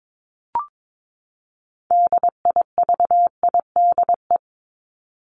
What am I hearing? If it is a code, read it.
Morse code, heard as a keyed tone, about 22 wpm, 700 Hz. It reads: DIVIDE
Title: Top 1501-1750 Words 22wpm